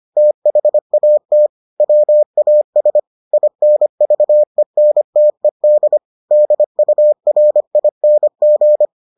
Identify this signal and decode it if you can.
Morse code, heard as a keyed tone, about 25 wpm, 600 Hz. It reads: THAT WAS INVENTED DURING